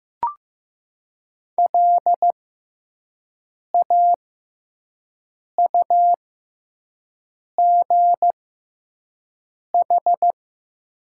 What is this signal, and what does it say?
Morse code, heard as a keyed tone, about 15 wpm, spaced as for 5 wpm, 700 Hz. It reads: LAUGH
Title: 6x Character Spacing - Top 500 Words - 15wpm